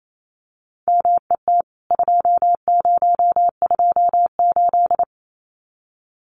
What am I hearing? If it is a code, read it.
Morse code, heard as a keyed tone, about 28 wpm, 700 Hz. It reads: MET 2028
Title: CWT Exchange 28wpm